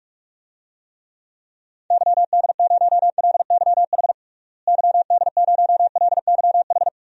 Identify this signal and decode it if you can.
Morse code, heard as a keyed tone, about 41 wpm, 700 Hz. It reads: YD0LYH YD0LYH
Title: DXCC-100 Call Signs - 6 Characters - Repeated 2x - 45wpm